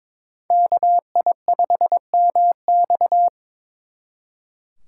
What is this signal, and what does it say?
Morse code, heard as a keyed tone, about 25 wpm, 700 Hz. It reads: KI5MX
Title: Speed Racing - USA 2x2 Call Signs 22wpm